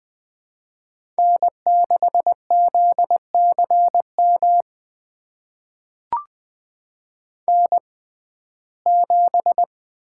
Morse code, heard as a keyed tone, about 21 wpm, spaced as for 15 wpm, 700 Hz. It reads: N6ZCM N7
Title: USA Call Signs - 6x Character Spacing - 20wpm